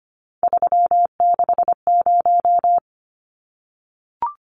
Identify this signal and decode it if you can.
Morse code, heard as a keyed tone, about 25 wpm, 690 Hz. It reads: 360
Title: Numbers - 3-Digit 25wpm